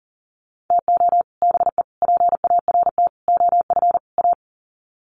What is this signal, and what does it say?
Morse code, heard as a keyed tone, about 40 wpm, 700 Hz. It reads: TO BE PART OF A
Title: Sets of 5 Words Encore2 40wpm